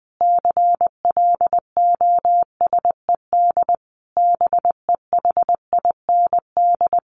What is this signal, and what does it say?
Morse code, heard as a keyed tone, about 20 wpm, 700 Hz. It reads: CLOSED BEHIND
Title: Sentences from Top 1400 Words 20wpm